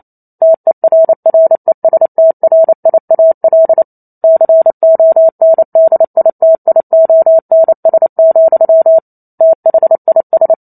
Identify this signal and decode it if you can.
Morse code, heard as a keyed tone, about 28 wpm, 650 Hz. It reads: TERRESTRIAL CONDITIONS, THIS
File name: Chapter 2 - 28 WPM